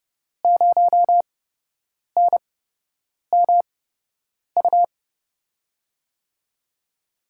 Morse code, heard as a keyed tone, about 30 wpm, spaced as for 8 wpm, 700 Hz. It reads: 0NMU
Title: USA Call Signs - 8x Character Spacing - 30wpm